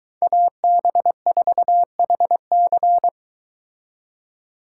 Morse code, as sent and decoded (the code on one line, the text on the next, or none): .- -... ....- .... -.-.
AB4HC